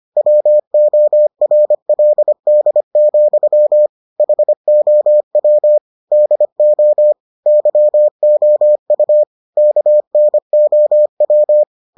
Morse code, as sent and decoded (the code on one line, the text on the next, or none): .-- --- .-. .-.. -.. --..-- / .... --- .-- / -.. --- / -.-- --- ..- / -.- -. --- .--
WORLD, HOW DO YOU KNOW